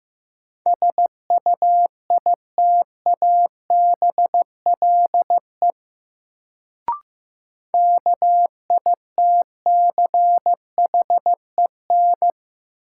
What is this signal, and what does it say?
Morse code, heard as a keyed tone, about 15 wpm, 700 Hz. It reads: SUITABLE KITCHEN
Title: Sentences from Top 1400 Words - New Word Drill - 15wpm